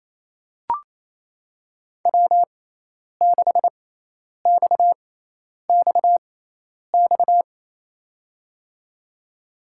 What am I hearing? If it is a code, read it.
Morse code, heard as a keyed tone, about 27 wpm, spaced as for 10 wpm, 700 Hz. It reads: W6XXX
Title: USA Call Signs - 6x Character Spacing - 28wpm